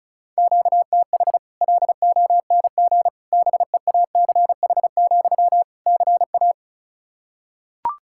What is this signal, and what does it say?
Morse code, heard as a keyed tone, about 34 wpm, 700 Hz. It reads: QTH LONG BEACH, CA
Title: Practice QSO - Exchange 1 - Beginning - 35wpm